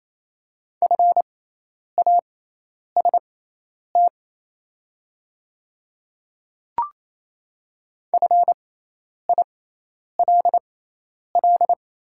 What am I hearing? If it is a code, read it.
Morse code, heard as a keyed tone, about 28 wpm, spaced as for 10 wpm, 690 Hz. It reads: FAST FILL
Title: 6x Character Spacing - Top 500 Words - 28wpm